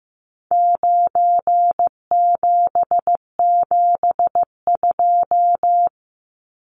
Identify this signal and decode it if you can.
Morse code, heard as a keyed tone, about 15 wpm, 700 Hz. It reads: 9772
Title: Numbers - 4-Digit 15wpm